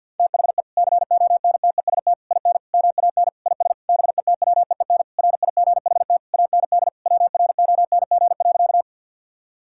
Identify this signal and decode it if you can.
Morse code, heard as a keyed tone, about 52 wpm, 700 Hz. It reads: THE CONTEST IN MAN IS BETWEEN RIGHT AND WRONG.
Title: Sentences from Top 1600 Words 50wpm